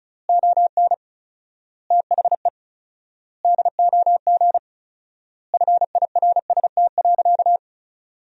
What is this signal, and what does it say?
Morse code, heard as a keyed tone, about 36 wpm, 700 Hz. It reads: ON THE DOG FIRST.
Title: 4x Word-Spacing - Sentences from Top 400 Words - 35wpm